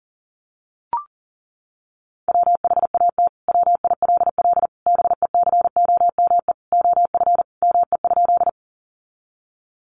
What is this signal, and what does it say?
Morse code, heard as a keyed tone, about 40 wpm, 700 Hz. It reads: WHAT WILL BECOME OF ME?
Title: Sentences from Top 600 Words 40wpm